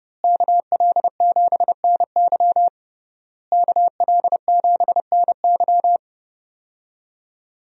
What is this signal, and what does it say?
Morse code, heard as a keyed tone, about 30 wpm, 700 Hz. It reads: KL7NY KL7NY
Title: USA Call Signs - 5 Characters - Repeated 3x - 30wpm